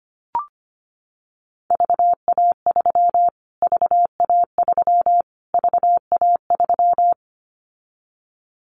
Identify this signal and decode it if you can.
Morse code, heard as a keyed tone, about 25 wpm, 700 Hz. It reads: VA3 VA3 VA3
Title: Callsign Prefixes for Top 25 DXCC - Repeated 3x - 25wpm